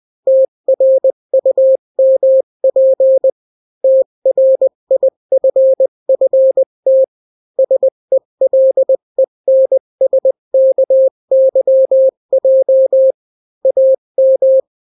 Morse code, heard as a keyed tone, about 20 wpm, 530 Hz. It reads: TRUMP TRIFFT SELENSKYJ AM
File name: daily / 18wpm